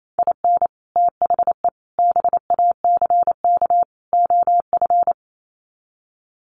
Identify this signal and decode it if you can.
Morse code, heard as a keyed tone, about 28 wpm, 700 Hz. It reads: IN THE BACK OF
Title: Sets of 4 Words 28wpm